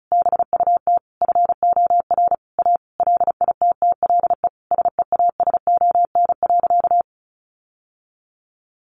Morse code, heard as a keyed tone, about 34 wpm, 700 Hz. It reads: BUT FOR A LITTLE SEASON.
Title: Sentences from Top 1400 Words 35wpm